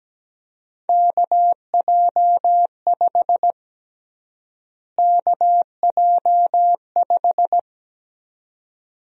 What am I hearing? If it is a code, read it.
Morse code, heard as a keyed tone, about 17 wpm, 700 Hz. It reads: KJ5 KJ5
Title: USA Call Sign Prefixes - Repeated 2x - 17wpm